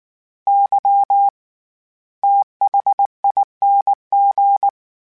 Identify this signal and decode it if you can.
Morse code, heard as a keyed tone, about 19 wpm, spaced as for 15 wpm, 800 Hz. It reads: YTHING
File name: Special INGs / Special INGs-mp3s / SlowToFaster ing @18